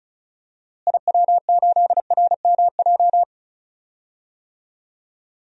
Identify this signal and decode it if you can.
Morse code, heard as a keyed tone, about 34 wpm, 700 Hz. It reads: IW8RMJ
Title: DXCC-100 Call Signs - 6 Characters - Repeated 2x - 35wpm